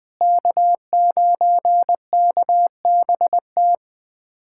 Morse code, heard as a keyed tone, about 20 wpm, 700 Hz. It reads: K9KBT
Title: USA Callsign DE Callsign - 5 Characters - 20wpm